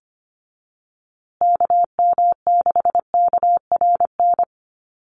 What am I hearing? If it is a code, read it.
Morse code, heard as a keyed tone, about 25 wpm, 700 Hz. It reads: KM6KRN
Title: USA Call Signs Repeated 2x 25wpm